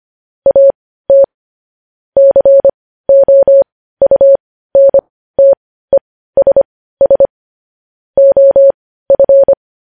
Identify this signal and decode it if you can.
Morse code, heard as a keyed tone, about 25 wpm, spaced as for 15 wpm, 550 Hz. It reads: AT COUNTESS OF
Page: news-headlines-in-morse-code-at-15-wpm